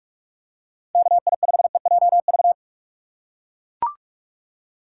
Morse code, heard as a keyed tone, about 40 wpm, 700 Hz. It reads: KI5EJV